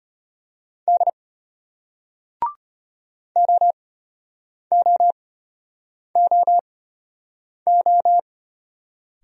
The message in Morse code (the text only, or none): D OOOO